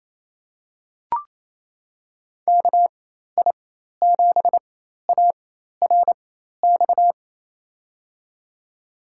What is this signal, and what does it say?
Morse code, heard as a keyed tone, about 28 wpm, spaced as for 13 wpm, 700 Hz. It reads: KI7ARX
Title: USA Call Signs - 6 Characters - 4x Character Spacing - 28wpm